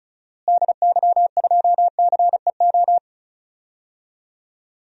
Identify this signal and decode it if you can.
Morse code, heard as a keyed tone, about 36 wpm, 700 Hz. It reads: DY2CEO
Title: DXCC-100 Call Signs - 6 Characters - Repeated 3x - 35wpm